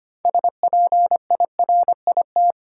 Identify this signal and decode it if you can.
Morse code, heard as a keyed tone, about 25 wpm, 700 Hz. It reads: SPIRIT
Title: Sentences from Top 1400 Words - New Word Drill - 25wpm